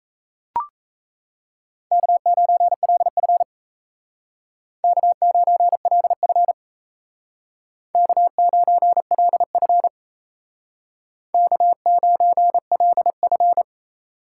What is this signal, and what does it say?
Morse code, heard as a keyed tone, about 35 wpm, 700 Hz. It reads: K9LF K9LF K9LF K9LF
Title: Speed Racing - USA 1x2 Call Signs 28wpm